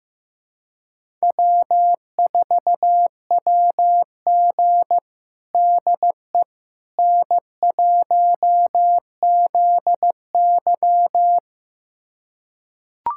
Morse code, heard as a keyed tone, about 15 wpm, 700 Hz. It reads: W4WG DE N1ZY
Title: USA Callsign DE Callsign - 4 Characters - 15wpm